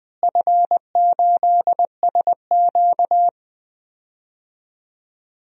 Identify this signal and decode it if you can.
Morse code, heard as a keyed tone, about 20 wpm, 700 Hz. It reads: F8SQ
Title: DXCC-100 Call Signs - 4 Characters - Repeated 2x - 20wpm